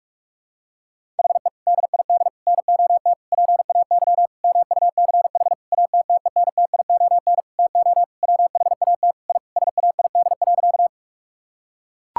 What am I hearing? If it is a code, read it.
Morse code, heard as a keyed tone, about 41 wpm, 700 Hz. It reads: HE DID NOT PAY MUCH ATTENTION TO WHAT I SAID.
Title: Sentences from Top 1000 Words - Review Part 6 - 45wpm